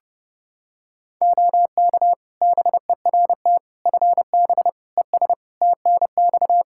Morse code, heard as a keyed tone, about 30 wpm, 700 Hz. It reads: OK BERT FB ES TNX
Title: Practice QSO - Exchange 2 - Info - 30wpm